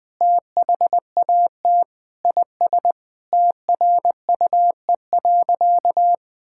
Morse code, heard as a keyed tone, about 20 wpm, 700 Hz. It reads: THAT IS TRUE.
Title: Sentences with 4-Word Phrases - Spoken Beginning - 20wpm